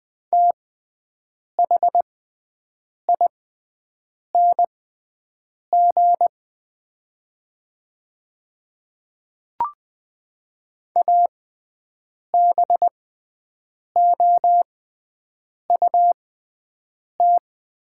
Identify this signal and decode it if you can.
Morse code, heard as a keyed tone, about 20 wpm, spaced as for 7 wpm, 700 Hz. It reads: THING ABOUT
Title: 6x Character Spacing - Top 100 Words - 20wpm